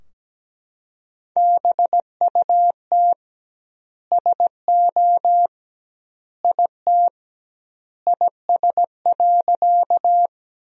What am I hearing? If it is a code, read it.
Morse code, heard as a keyed tone, about 17 wpm, 700 Hz. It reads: BUT SO IT IS.
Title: 2x Word-Spacing - Sentences from Top 100 Words - 17wpm